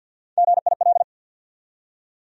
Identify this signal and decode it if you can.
Morse code, heard as a keyed tone, about 53 wpm, 700 Hz. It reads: GIL
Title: USA Call Sign Postfixes - 3 Letters 50wpm